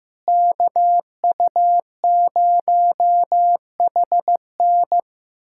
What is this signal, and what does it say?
Morse code, heard as a keyed tone, about 15 wpm, 700 Hz. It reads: KU0HN
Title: USA Callsign DE Callsign - 5 Characters - 15wpm